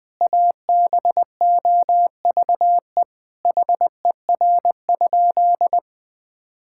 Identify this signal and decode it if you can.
Morse code, heard as a keyed tone, about 20 wpm, 700 Hz. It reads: ABOVE HER?
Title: Sentences from Top 1000 Words - Review Part 5 - 20wpm